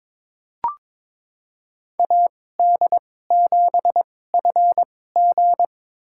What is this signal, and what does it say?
Morse code, heard as a keyed tone, about 22 wpm, spaced as for 16 wpm, 700 Hz. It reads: AD7FG
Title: USA Call Signs - 2x Character Spacing - 22wpm